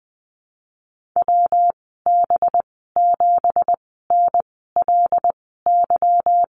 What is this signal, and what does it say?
Morse code, heard as a keyed tone, about 20 wpm, spaced as for 15 wpm, 700 Hz. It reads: WB7NLY
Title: 219 Lesson 35g - Number 0 - Call Signs 20wpm